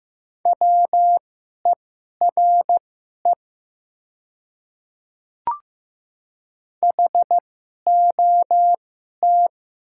Morse code, heard as a keyed tone, about 15 wpm, spaced as for 11 wpm, 700 Hz. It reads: WERE HOT
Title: 2x Character Spacing - Top 500 Words - 15wpm